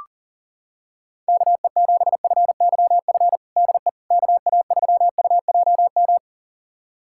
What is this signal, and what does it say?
Morse code, heard as a keyed tone, about 40 wpm, 700 Hz. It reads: KE7FYF DE KA3UJM
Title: USA Callsign DE Callsign - 6 Characters - 40wpm